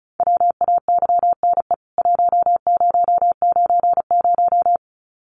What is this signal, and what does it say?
Morse code, heard as a keyed tone, about 34 wpm, 700 Hz. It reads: WAYNE 1090
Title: CWT Exchange 35wpm